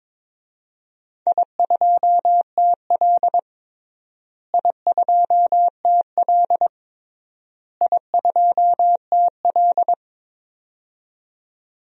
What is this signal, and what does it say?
Morse code, heard as a keyed tone, about 22 wpm, 700 Hz. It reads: I2TL I2TL I2TL
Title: DXCC-100 Call Signs - 4 Characters - Repeated 3x - 22wpm